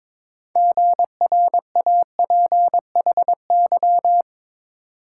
Morse code, heard as a keyed tone, about 27 wpm, 700 Hz. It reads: GRAPHY
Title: Speed-Racing Multi-letter Phonemes - 22wpm